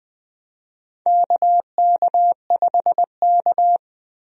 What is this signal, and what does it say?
Morse code, heard as a keyed tone, about 20 wpm, 700 Hz. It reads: KK5K
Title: USA Call Signs - 4 Characters - Repeated 2x - 20wpm